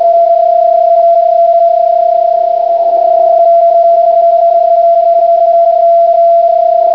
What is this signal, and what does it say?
PSK31, about 31.25 baud.